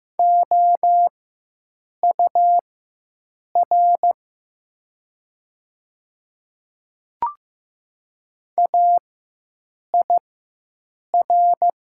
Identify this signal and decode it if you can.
Morse code, heard as a keyed tone, about 15 wpm, spaced as for 7 wpm, 700 Hz. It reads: OUR AIR
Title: 4x Character Spacing - Top 200 Words - 15wpm